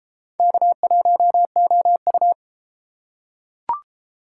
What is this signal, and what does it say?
Morse code, heard as a keyed tone, about 33 wpm, 700 Hz. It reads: K1OU